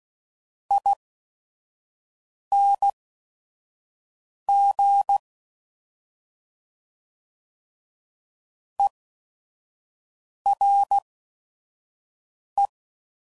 Morse code, heard as a keyed tone, about 16 wpm, spaced as for 5 wpm, 800 Hz. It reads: ING ERE
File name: Suffixes / General / ERE INGs